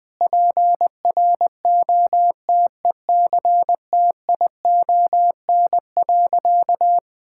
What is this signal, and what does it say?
Morse code, heard as a keyed tone, about 20 wpm, 700 Hz. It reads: PROTECTION.
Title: Sentences from Top 1600 Words 20wpm